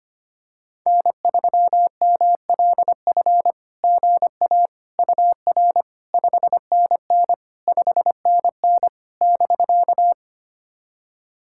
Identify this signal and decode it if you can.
Morse code, heard as a keyed tone, about 25 wpm, 660 Hz. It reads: N3MLF GA UR 5NN 5NN <BK>
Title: SOTA QSO Practice 25wpm